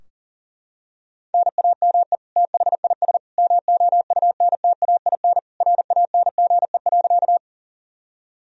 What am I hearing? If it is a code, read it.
Morse code, heard as a keyed tone, about 37 wpm, 700 Hz. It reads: NAME THIS MOUNTAIN RANGE.
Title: Sentences from Top 1000 Words - Review Part 4 - 40wpm